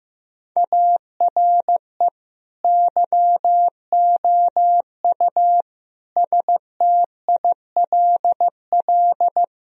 Morse code, heard as a keyed tone, about 15 wpm, 700 Hz. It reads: ARE YOU STILL
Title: Sentences from Top 1000 Words - Review Part 2 - 15wpm